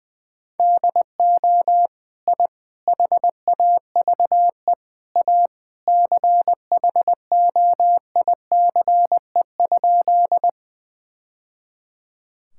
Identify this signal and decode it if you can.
Morse code, heard as a keyed tone, about 20 wpm, 700 Hz. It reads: DO I HAVE A CHOICE?
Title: Sentences from Top 1000 Words - Review Part 6 - 20wpm